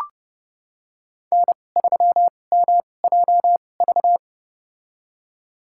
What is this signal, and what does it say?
Morse code, heard as a keyed tone, about 30 wpm, spaced as for 22 wpm, 700 Hz. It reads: N3MJV
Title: USA Call Signs - 5 Characters - 2x Character Spacing - 30wpm